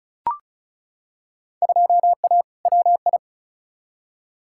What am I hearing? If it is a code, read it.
Morse code, heard as a keyed tone, about 36 wpm, 700 Hz. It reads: 2A WI
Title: Field Day Exchanges - 35wpm